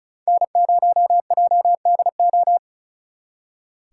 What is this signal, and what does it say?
Morse code, heard as a keyed tone, about 45 wpm, 700 Hz. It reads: N0JDO